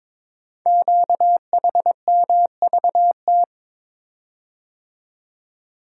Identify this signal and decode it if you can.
Morse code, heard as a keyed tone, about 23 wpm, 700 Hz. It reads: QHMVT
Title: Code Groups - 5 Letters - 22wpm